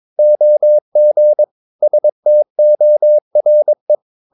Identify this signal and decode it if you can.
Morse code, heard as a keyed tone, about 22 wpm, 600 Hz. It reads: OG STORE
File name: Ferie QRS 22wpm